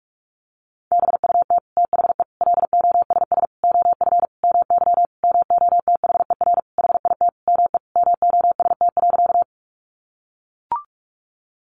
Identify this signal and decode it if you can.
Morse code, heard as a keyed tone, about 44 wpm, 700 Hz. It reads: BUT THE LOSS OF MY MOTHER HIT ME MOST.
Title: Sentences from Top 1400 Words 45wpm